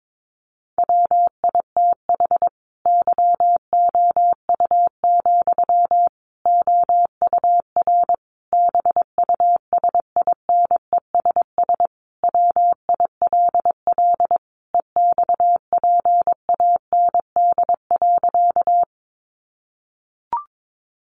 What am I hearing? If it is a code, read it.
Morse code, heard as a keyed tone, about 22 wpm, 700 Hz. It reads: WITH YOU, OUR BUSINESS WILL EXPAND.
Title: Sentences from Top 1400 Words 22wpm